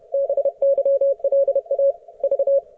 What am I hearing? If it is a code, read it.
Morse code, heard as a keyed tone, about 31 wpm, 550 Hz. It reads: BYLA V
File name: Archiv / CUC138 / ok1fhi cul138